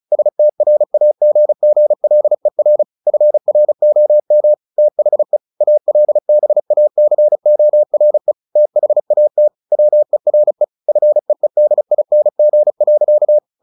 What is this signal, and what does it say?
Morse code, heard as a keyed tone, about 34 wpm, 600 Hz. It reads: STRAGGLER FROM THE ALBACORE THAT WERE FEEDING.
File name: the old man and the sea / the old man and the sea 1 35wpm